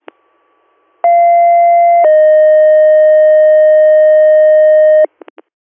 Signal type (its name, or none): two-tone page